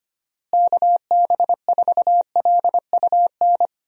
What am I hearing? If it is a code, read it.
Morse code, heard as a keyed tone, about 25 wpm, 700 Hz. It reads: KB4LUN
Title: USA Call Signs - 6 Characters - Repeated 2x - 25wpm